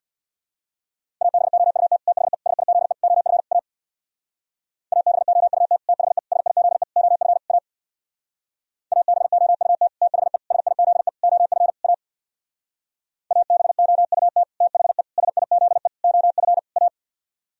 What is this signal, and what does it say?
Morse code, heard as a keyed tone, about 62 wpm, 700 Hz. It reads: ABOUT THE SIZE OF A ABOUT THE SIZE OF A ABOUT THE SIZE OF A ABOUT THE SIZE OF A
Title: Speed-Racing - Sets of 5 Words 50wpm